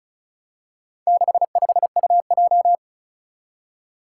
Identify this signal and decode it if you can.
Morse code, heard as a keyed tone, about 34 wpm, 700 Hz. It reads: 65UJ
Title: Four Challenging Letter-Number Combinations 35wpm